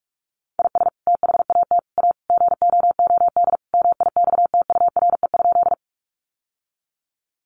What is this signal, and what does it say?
Morse code, heard as a keyed tone, about 44 wpm, 700 Hz. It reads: IS THAT A GOOD MIXTURE?
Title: Sentences from Top 1500 Words 45wpm